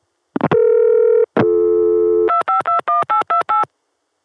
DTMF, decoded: 2521828